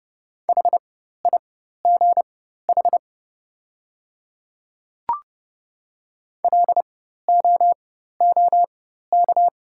Morse code, heard as a keyed tone, about 30 wpm, spaced as for 14 wpm, 700 Hz. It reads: HIGH LOOK
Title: 4x Character Spacing - Top 500 Words - 30wpm